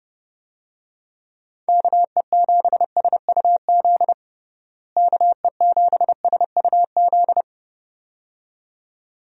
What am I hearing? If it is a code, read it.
Morse code, heard as a keyed tone, about 30 wpm, 700 Hz. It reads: KE7SUZ KE7SUZ
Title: USA Call Signs - 6 Characters - Repeated 2x - 30wpm